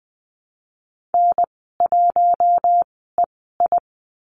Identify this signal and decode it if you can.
Morse code, heard as a keyed tone, about 20 wpm, spaced as for 15 wpm, 700 Hz. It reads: N1EI
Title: 062 Lesson 11g - Letter h - Call Signs 20wpm